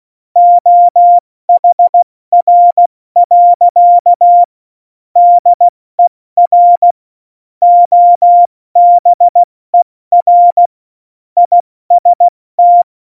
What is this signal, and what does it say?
Morse code, heard as a keyed tone, about 16 wpm, 700 Hz. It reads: OHR. DER OBER IST
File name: Morsekurs / Witze / Witz 16. April 2019, 70 F80